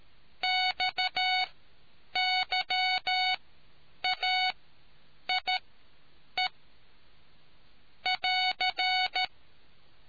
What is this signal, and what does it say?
Morse code, heard as a keyed tone, about 13 wpm, spaced as for 8 wpm, 740 Hz. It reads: XYAIE <AR>